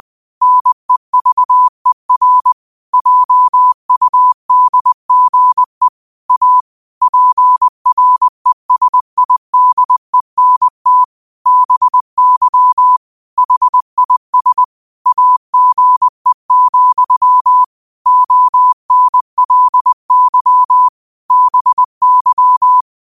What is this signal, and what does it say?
Morse code, heard as a keyed tone, about 20 wpm, 1000 Hz. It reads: NEVER JUDGE A PRESIDENT BY HIS AGE, ONLY BY